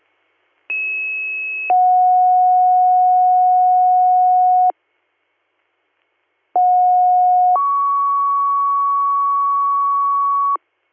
Two-tone page, heard as a one-second tone, then a three-second tone, about 2600 then 730 Hz, about 730 then 1120 Hz. It is heard from 0.7 to 10.6 s.